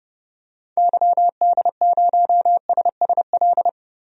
Morse code, heard as a keyed tone, about 31 wpm, 700 Hz. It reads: YD0SSL
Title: DXCC-100 Call Signs - 6 Characters - Repeated 2x - 30wpm